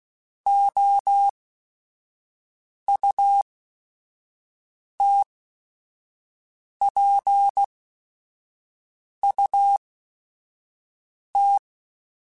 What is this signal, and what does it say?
Morse code, heard as a keyed tone, about 16 wpm, spaced as for 5 wpm, 800 Hz. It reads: OUTPUT
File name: Technical Subjects